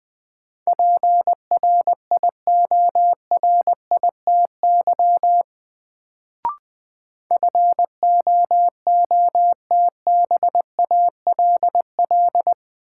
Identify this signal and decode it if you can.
Morse code, heard as a keyed tone, about 21 wpm, 700 Hz. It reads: PRIORITY FOOTBALL
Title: Sentences from Top 1600 Words - New Word Drill - 20wpm